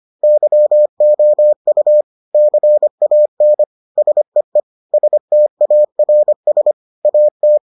Morse code, heard as a keyed tone, about 25 wpm, 600 Hz. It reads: YOU CAN SEE STARS AT